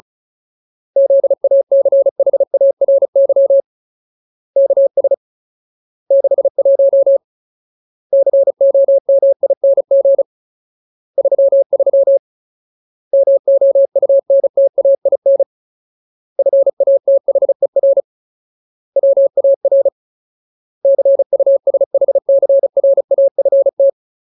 Morse code, heard as a keyed tone, about 34 wpm, 550 Hz. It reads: ZACHARY KS 61 COMING 33 MOUNTAIN FATHER WAR CUSHCRAFT
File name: Ultimate Word List / 35WPM Ultimate